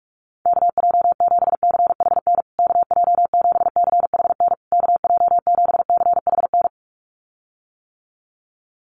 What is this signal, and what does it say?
Morse code, heard as a keyed tone, about 44 wpm, 700 Hz. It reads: KJ7CHN KJ7CHN KJ7CHN
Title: USA Call Signs Repeated 3x 45wpm